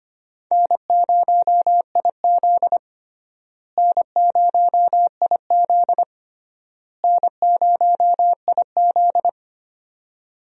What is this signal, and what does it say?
Morse code, heard as a keyed tone, about 25 wpm, 700 Hz. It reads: N0IZ N0IZ N0IZ
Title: USA Call Signs - 4 Characters - Repeated 3x - 25wpm